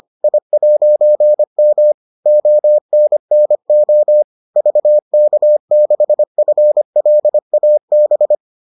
Morse code, heard as a keyed tone, about 25 wpm, 600 Hz. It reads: I'M ONNO VK6FLAB